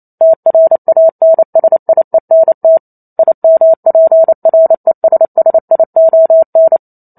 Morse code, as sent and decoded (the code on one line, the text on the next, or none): - .-. .- -. ... .. -. - / .. -- .--. .-. ... ... .. --- -.
TRANSIENT IMPRESSION